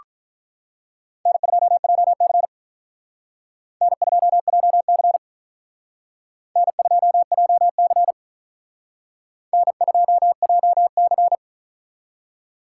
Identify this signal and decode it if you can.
Morse code, heard as a keyed tone, about 45 wpm, 700 Hz. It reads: N2JC N2JC N2JC N2JC